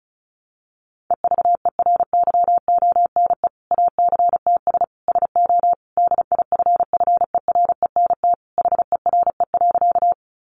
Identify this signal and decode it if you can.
Morse code, heard as a keyed tone, about 34 wpm, 700 Hz. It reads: EVERYONE ACTS SO DIFFERENT HERE.
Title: Easy Phrases 35wpm